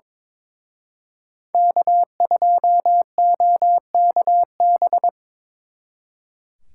Morse code, heard as a keyed tone, about 24 wpm, 700 Hz. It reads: K2OKB
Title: Speed Racing - USA 1x3 Call Signs 22wpm